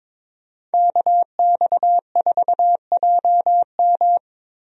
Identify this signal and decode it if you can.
Morse code, heard as a keyed tone, about 22 wpm, 700 Hz. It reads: KX4JM
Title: USA Call Signs - 5 Characters - Repeated 3x - 22wpm